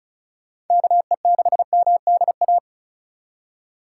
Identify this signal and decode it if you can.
Morse code, heard as a keyed tone, about 34 wpm, 700 Hz. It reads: KE6MDA
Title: USA Call Signs - 2x Character Spacing - 35wpm